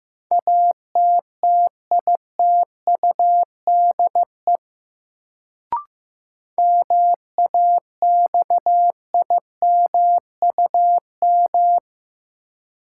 Morse code, heard as a keyed tone, about 15 wpm, 700 Hz. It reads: ATTITUDE MAXIMUM
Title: Sentences from Top 1300 Words - New Word Drill Alt - 15wpm